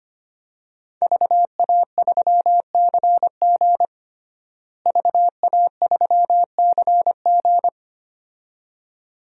Morse code, heard as a keyed tone, about 25 wpm, 700 Hz. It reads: VA3CG VA3CG
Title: DXCC-100 Call Signs - 5 Characters - Repeated 2x - 25wpm